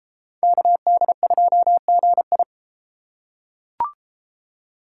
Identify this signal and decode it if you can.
Morse code, heard as a keyed tone, about 32 wpm, 700 Hz. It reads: KD2GI